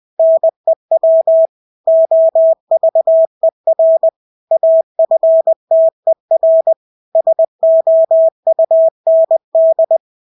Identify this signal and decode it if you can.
Morse code, heard as a keyed tone, about 20 wpm, 650 Hz. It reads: NEW OVER AFTER SOUND
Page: Engelske ord 20 wpm